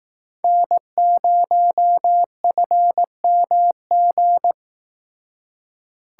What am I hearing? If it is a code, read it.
Morse code, heard as a keyed tone, about 18 wpm, 700 Hz. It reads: N0FMG
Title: Speed Racing - USA 1x3 Call Signs 15wpm